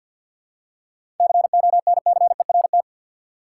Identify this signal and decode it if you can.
Morse code, heard as a keyed tone, about 52 wpm, 700 Hz. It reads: CONCERT
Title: Seven Letter Words - 50wpm